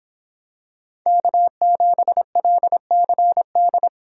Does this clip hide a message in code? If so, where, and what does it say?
Yes: Morse code from 1.1 to 3.9 s, heard as a keyed tone, about 26 wpm, 700 Hz. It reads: K7LCD